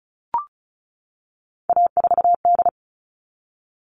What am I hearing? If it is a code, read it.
Morse code, heard as a keyed tone, about 34 wpm, 700 Hz. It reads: A4D
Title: All Cut Numbers - 3-Digit 35wpm